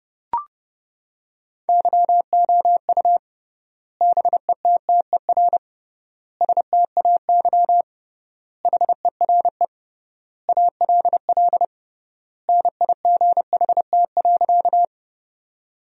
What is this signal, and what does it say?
Morse code, heard as a keyed tone, about 32 wpm, 700 Hz. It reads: YOU BETTER STAY HERE ALL NIGHT.
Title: 3x Word-Spacing - Sentences from Top 500 Words - 30wpm